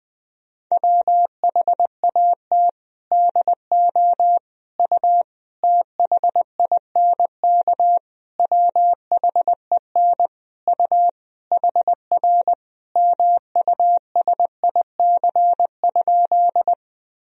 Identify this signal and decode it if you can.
Morse code, heard as a keyed tone, about 20 wpm, 700 Hz. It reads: WHAT DO U THINK WHEN U HR MUSIC?
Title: Sentences from Top 300 Words and Abbreviations 20wpm